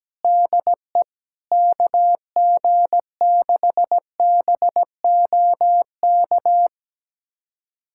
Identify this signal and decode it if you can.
Morse code, heard as a keyed tone, about 17 wpm, 700 Hz. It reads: DE KG6BOK
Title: USA Callsign DE Callsign - 6 Characters - 17wpm